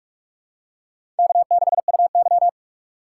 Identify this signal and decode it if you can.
Morse code, heard as a keyed tone, about 44 wpm, 690 Hz. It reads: K6UY
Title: USA Call Signs - 4 Characters - 4x Character Spacing - 45wpm